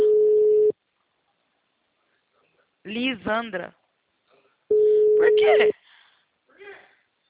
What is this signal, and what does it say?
Ringback tone (Europe, 425 Hz), with speech.